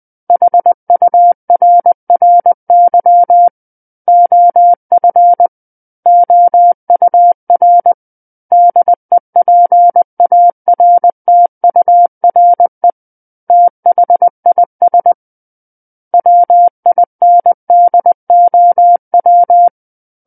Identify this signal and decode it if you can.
Morse code, heard as a keyed tone, about 20 wpm, 700 Hz. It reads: HURRY OF OUR DEPARTURE THIS WINDOW
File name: 20 WPM